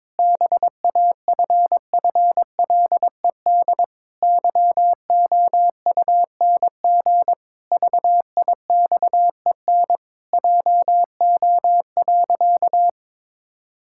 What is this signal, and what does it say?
Morse code, heard as a keyed tone, about 22 wpm, 700 Hz. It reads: BAFFLED YOUNG VIXEN JO.
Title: Pangram Challenge v2 - 22wpm